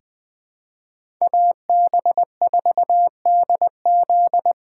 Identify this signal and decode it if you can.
Morse code, heard as a keyed tone, about 21 wpm, 700 Hz. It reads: AB4DZ
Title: USA Call Signs - 5 Characters - 3x Character Spacing - 20wpm